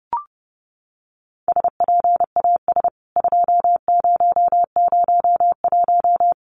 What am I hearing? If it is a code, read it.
Morse code, heard as a keyed tone, about 30 wpm, 700 Hz. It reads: SPAS 2001
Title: CWT Exchange 30wpm